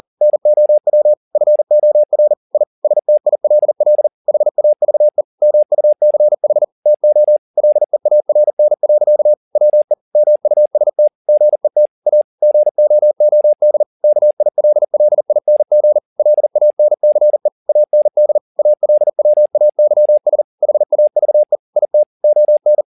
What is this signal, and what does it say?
Morse code, heard as a keyed tone, about 40 wpm, 600 Hz. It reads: NOW FOR I STILL HAVE MUCH TO LEARN. WE MUST GET A GOOD KILLING LANCE AND ALWAYS HAVE IT ON
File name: the old man and the sea 2 40wpm